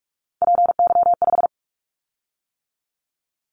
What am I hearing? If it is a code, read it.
Morse code, heard as a keyed tone, about 46 wpm, 700 Hz. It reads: PY5
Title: Callsign Prefixes for Top 100 DXCC - 45wpm